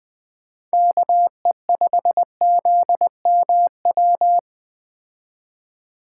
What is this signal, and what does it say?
Morse code, heard as a keyed tone, about 20 wpm, 700 Hz. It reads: KE5ZMW